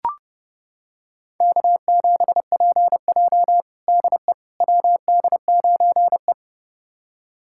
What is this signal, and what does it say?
Morse code, heard as a keyed tone, about 30 wpm, 700 Hz. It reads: K7PJ DE WD9E
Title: USA Callsign DE Callsign - 4 Characters - 30wpm